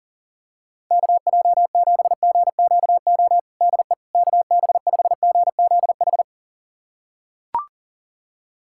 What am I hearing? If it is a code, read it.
Morse code, heard as a keyed tone, about 40 wpm, 700 Hz. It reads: KJ7GQO DE KB5GZH
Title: USA Callsign DE Callsign - 6 Characters - 40wpm